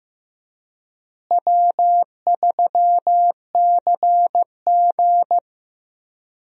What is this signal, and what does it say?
Morse code, heard as a keyed tone, about 15 wpm, 700 Hz. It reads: W3CG
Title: USA Call Signs - 4 Characters - Repeated 3x - 15wpm